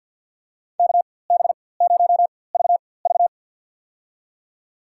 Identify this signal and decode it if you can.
Morse code, heard as a keyed tone, about 50 wpm, spaced as for 23 wpm, 700 Hz. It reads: KB0VV
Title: USA Call Signs - 4x Character Spacing - 50wpm